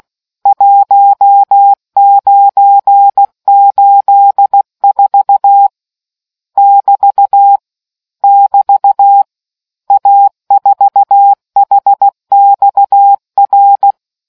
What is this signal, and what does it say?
Morse code, heard as a keyed tone, about 16 wpm, 800 Hz. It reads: 1984 <BT> <BT> A4HXR